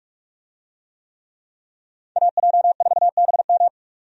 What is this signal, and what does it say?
Morse code, heard as a keyed tone, about 47 wpm, 700 Hz. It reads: AJ4BM
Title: USA Call Signs - 5 Characters - 45wpm